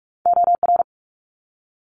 Morse code, heard as a keyed tone, about 44 wpm, 700 Hz. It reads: OR
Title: ARRL Sections 45wpm